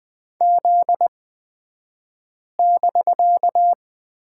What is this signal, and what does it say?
Morse code, heard as a keyed tone, about 20 wpm, spaced as for 5 wpm, 700 Hz. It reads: Z<BK>
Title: 249 Lesson 40d - BK - Two Characters 20wpm